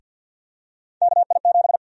Morse code, heard as a keyed tone, about 52 wpm, 700 Hz. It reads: KI7
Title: USA Call Sign Prefixes - Repeated 2x - 50wpm